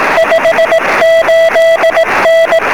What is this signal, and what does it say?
Morse code, heard as a keyed tone, about 18 wpm, 660 Hz, with noise. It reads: 58N
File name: sm7lcb / 12-sm7ecm 23cm